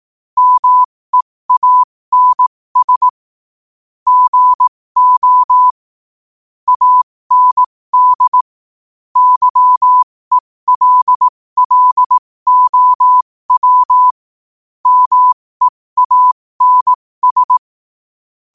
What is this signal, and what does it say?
Morse code, heard as a keyed tone, about 18 wpm, spaced as for 15 wpm, 1000 Hz. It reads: MEANS GO AND YELLOW MEANS
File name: qotdfiles / 15wpm